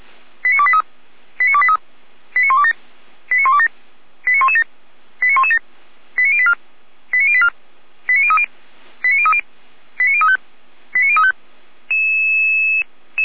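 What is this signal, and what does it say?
Five-tone selective call (ZVEI), heard as quick runs of five tones, tones of about 70 ms. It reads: 89283 89283 89127 89127 89117 89117 89084 89084 89030 89030 89035 89035
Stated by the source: ZVEI five-tone selective call received by SDR.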